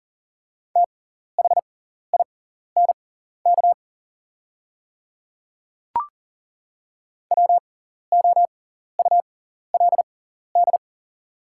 Morse code, heard as a keyed tone, about 42 wpm, spaced as for 14 wpm, 690 Hz. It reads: THINK WOULD
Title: 6x Character Spacing - Top 200 Words - 40wpm